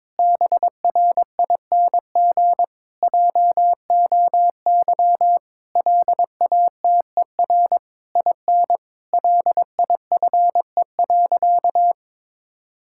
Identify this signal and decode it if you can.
Morse code, heard as a keyed tone, about 22 wpm, 700 Hz. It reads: BRING JOY LATER IN LIFE.
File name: Difficult Sentences with Hints - 22wpm